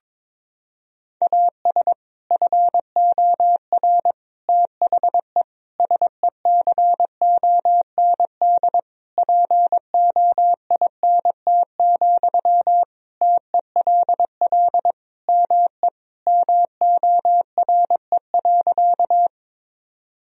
Morse code, heard as a keyed tone, about 22 wpm, 700 Hz. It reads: AS FOR THE SECOND POINT, TELL ME MORE.
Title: Sentences from Top 1000 Words - Review Part 1 - 22wpm